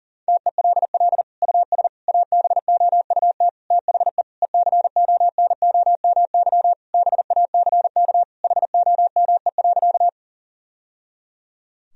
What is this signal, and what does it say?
Morse code, heard as a keyed tone, about 40 wpm, 700 Hz. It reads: TELL US ABOUT THE ECONOMY BACK HOME.
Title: Sentences from Top 1000 Words - Review Part 3 - 40wpm